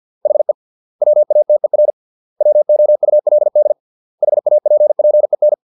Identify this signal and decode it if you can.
Morse code, heard as a keyed tone, about 50 wpm, 600 Hz. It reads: HE WATER WOULD HAPPEN